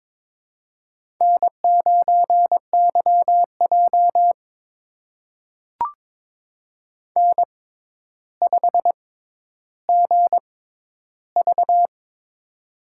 Morse code, heard as a keyed tone, about 22 wpm, spaced as for 11 wpm, 700 Hz. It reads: N9YJ N5GV
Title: USA Call Signs - 4 Characters - 6x Character Spacing - 22wpm